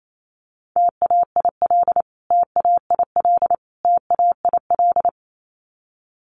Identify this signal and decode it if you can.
Morse code, heard as a keyed tone, about 28 wpm, 700 Hz. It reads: TAIL TAIL TAIL
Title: Top 500 Words Repeated 28wpm